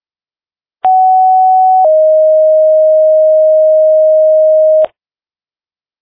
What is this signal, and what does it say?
Two-tone page, heard as a one-second tone, then a three-second tone, about 750 then 620 Hz. It is heard from 0.8 to 4.9 s.